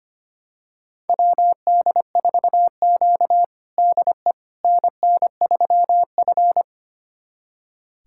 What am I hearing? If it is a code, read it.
Morse code, heard as a keyed tone, about 25 wpm, 700 Hz. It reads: WD4Q DE NN3F
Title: USA Callsign DE Callsign - 4 Characters - 25wpm